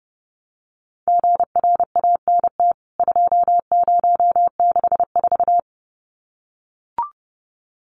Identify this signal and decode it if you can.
Morse code, heard as a keyed tone, about 30 wpm, 700 Hz. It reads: GRANT 2064
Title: CWT Exchange 30wpm